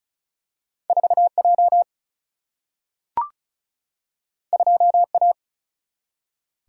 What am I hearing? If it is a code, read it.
Morse code, heard as a keyed tone, about 36 wpm, 700 Hz. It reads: 4 J 2 A
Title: Single-Letter Single-Number 35wpm